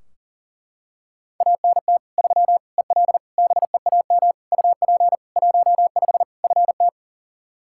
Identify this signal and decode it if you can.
Morse code, heard as a keyed tone, about 40 wpm, 700 Hz. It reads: ANT 3 EL BEAM UP 15 FT
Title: Practice QSO - Exchange 2 - Info - 40wpm